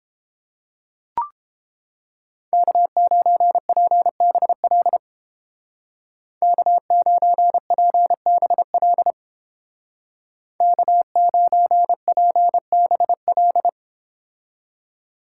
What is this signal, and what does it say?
Morse code, heard as a keyed tone, about 31 wpm, 700 Hz. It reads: K9PBL K9PBL K9PBL